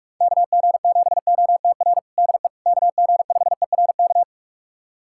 Morse code, heard as a keyed tone, about 43 wpm, 690 Hz. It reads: KG7OTR DE KG5ERK
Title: USA Callsign DE Callsign - 6 Characters - 45wpm